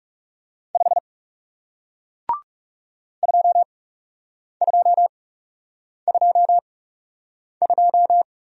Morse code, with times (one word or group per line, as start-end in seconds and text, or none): 0.8-1.0 5
3.2-8.2 2222